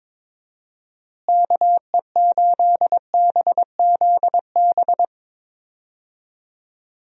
Morse code, heard as a keyed tone, about 22 wpm, 700 Hz. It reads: KE8BZB
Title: USA Call Signs - 6 Characters - Repeated 3x - 22wpm